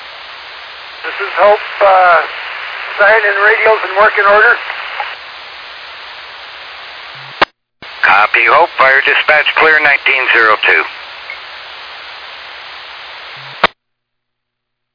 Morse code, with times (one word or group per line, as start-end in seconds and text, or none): none